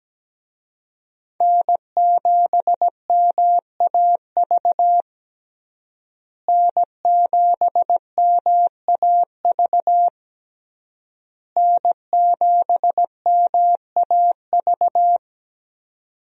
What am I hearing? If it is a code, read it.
Morse code, heard as a keyed tone, about 17 wpm, 700 Hz. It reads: N7MAV N7MAV N7MAV
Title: USA Call Signs - 5 Characters - Repeated 3x - 17wpm